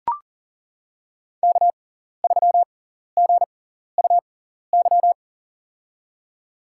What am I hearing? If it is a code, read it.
Morse code, heard as a keyed tone, about 40 wpm, spaced as for 14 wpm, 700 Hz. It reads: K3GUY
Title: USA Call Signs - 6x Character Spacing - 40wpm